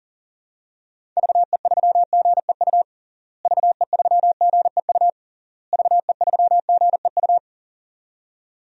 Morse code, heard as a keyed tone, about 40 wpm, 700 Hz. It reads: VE3GEU VE3GEU VE3GEU
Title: DXCC-100 Call Signs - 6 Characters - Repeated 3x - 40wpm